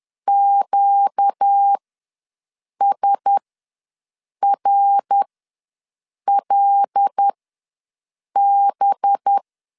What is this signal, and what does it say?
Morse code, heard as a keyed tone, about 10 wpm, spaced as for 6 wpm, 800 Hz. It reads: QSRLB
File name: LEZIONI / 5 LBVC / 5 LB ws 10wpm 3616